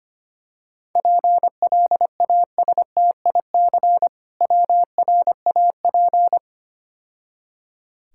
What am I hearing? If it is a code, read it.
Morse code, heard as a keyed tone, about 25 wpm, 700 Hz. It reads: PLASTIC WRAP
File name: Easy Words - Speed-Racing with Hints - 25wpm